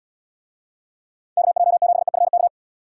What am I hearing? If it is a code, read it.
Morse code, heard as a keyed tone, about 70 wpm, 700 Hz. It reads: KJ7EFK